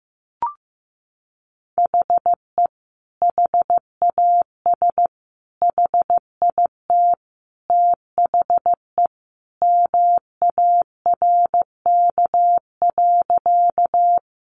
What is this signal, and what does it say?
Morse code, heard as a keyed tone, about 15 wpm, 700 Hz. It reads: HE HAS HIT THE MARK.
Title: Sentences from Top 1200 Words 15wpm